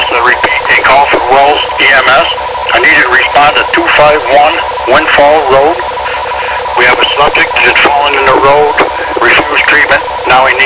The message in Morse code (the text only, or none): none